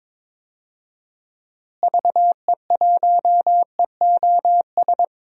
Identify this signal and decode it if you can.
Morse code, heard as a keyed tone, about 22 wpm, 700 Hz. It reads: VE1EOS
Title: DXCC-100 Call Signs - 6 Characters - Repeated 3x - 22wpm